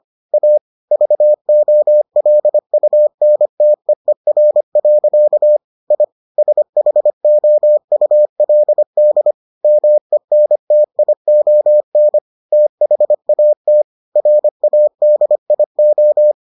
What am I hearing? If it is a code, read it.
Morse code, heard as a keyed tone, about 25 wpm, 600 Hz. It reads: A VOLUNTEER. I SHOULD MENTION THAT RADIO